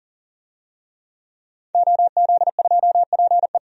Morse code, heard as a keyed tone, about 40 wpm, 700 Hz. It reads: OZ2PE
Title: DXCC-100 Call Signs - 5 Characters - 40wpm